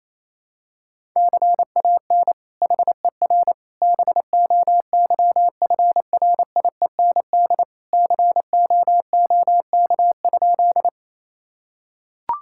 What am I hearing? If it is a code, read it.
Morse code, heard as a keyed tone, about 27 wpm, 700 Hz. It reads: CAN HER BOYFRIEND COOK?
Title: Sentences from Top 1000 Words - Review Part 5 - 28wpm